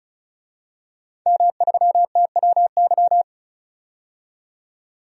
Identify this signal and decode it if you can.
Morse code, heard as a keyed tone, about 36 wpm, 700 Hz. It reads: M3TWY
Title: DXCC-100 Call Signs - 5 Characters - 35wpm